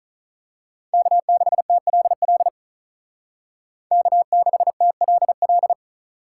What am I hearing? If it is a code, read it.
Morse code, heard as a keyed tone, about 36 wpm, 700 Hz. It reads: K6TLL K6TLL
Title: Speed Racing - USA Call Signs 35wpm